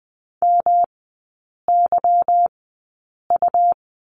Morse code, heard as a keyed tone, about 20 wpm, spaced as for 8 wpm, 690 Hz. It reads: MYU
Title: 145 Lesson 24e - Letter y - Three Characters 20wpm